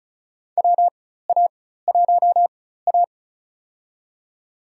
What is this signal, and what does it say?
Morse code, heard as a keyed tone, about 34 wpm, spaced as for 16 wpm, 700 Hz. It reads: WA1A
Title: USA Call Signs - 4 Characters - 4x Character Spacing - 35wpm